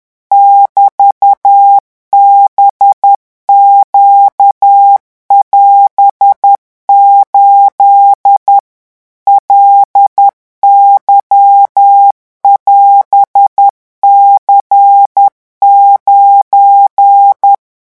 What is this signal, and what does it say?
Morse code, heard as a keyed tone, about 10 wpm, 800 Hz. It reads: <BT>BQ<AS>8 LY<AS>C9
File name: morse mp3 / revision19-vitesse 10 mots